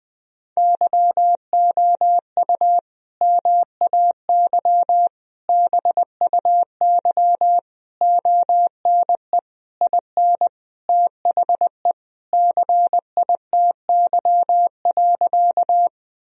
Morse code, heard as a keyed tone, about 20 wpm, 700 Hz. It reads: YOU MAY BUY ONE IN THE CITY.
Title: Sentences from Top 1000 Words - Review Part 4 - 20wpm